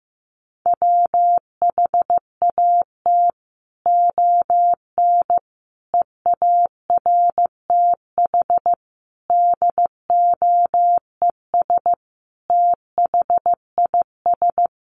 Morse code, heard as a keyed tone, about 15 wpm, 700 Hz. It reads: WHAT ON EARTH DOES THIS
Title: Sentences from Top 200 Words 15wpm